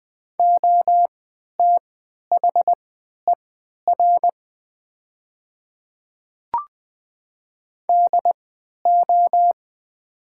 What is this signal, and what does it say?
Morse code, heard as a keyed tone, about 20 wpm, spaced as for 11 wpm, 700 Hz. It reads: OTHER DO
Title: 3x Character Spacing - Top 300 Words - 20wpm